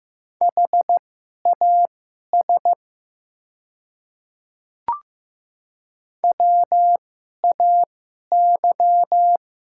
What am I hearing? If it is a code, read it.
Morse code, heard as a keyed tone, about 15 wpm, spaced as for 11 wpm, 700 Hz. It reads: HAS WAY
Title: 2x Character Spacing - Top 200 Words - 15wpm